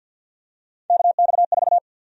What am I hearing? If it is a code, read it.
Morse code, heard as a keyed tone, about 53 wpm, 700 Hz. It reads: KX4
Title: USA Call Sign Prefixes - Repeated 2x - 50wpm